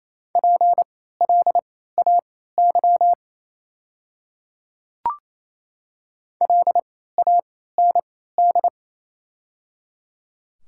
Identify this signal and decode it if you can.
Morse code, heard as a keyed tone, about 28 wpm, spaced as for 16 wpm, 700 Hz. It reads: PLAY LAND
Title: 3x Character Spacing - Top 300 Words - 28wpm